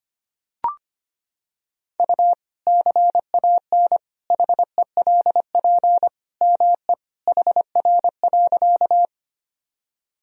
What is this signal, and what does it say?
Morse code, heard as a keyed tone, about 25 wpm, 700 Hz. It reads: U CAN HELP ME HR.
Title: Sentences from Top 200 Words and Abbreviations 25wpm